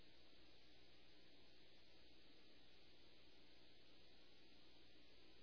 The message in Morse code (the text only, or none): none